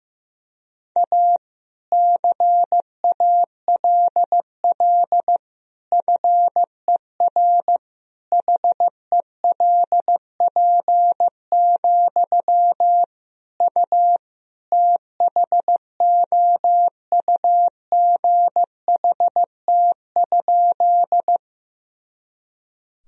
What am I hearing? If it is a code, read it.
Morse code, heard as a keyed tone, about 15 wpm, 700 Hz. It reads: A CALL FER HELP, U THOUGHT?
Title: Sentences from Top 300 Words and Abbreviations 15wpm